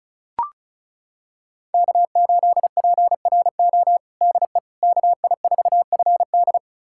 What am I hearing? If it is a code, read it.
Morse code, heard as a keyed tone, about 36 wpm, 700 Hz. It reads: K8PRO DE KI4FD
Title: USA Callsign DE Callsign - 5 Characters - 35wpm